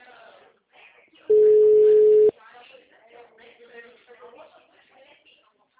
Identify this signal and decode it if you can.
Ringback tone (Europe, 425 Hz), with speech.